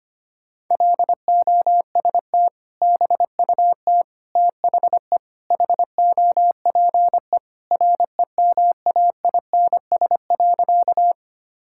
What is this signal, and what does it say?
Morse code, heard as a keyed tone, about 25 wpm, 700 Hz. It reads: LOST BUT THE HOPE REMAINS.
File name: practice / 25wpm / Sentences with 2-Word Phrases - 25wpm